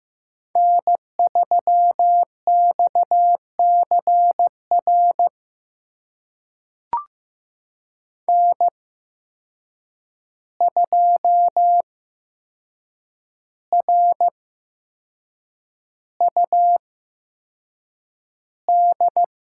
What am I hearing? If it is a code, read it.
Morse code, heard as a keyed tone, about 15 wpm, spaced as for 6 wpm, 700 Hz. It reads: N3XCR N2RUD
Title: USA Call Signs - 5 Characters - 8x Character Spacing - 15wpm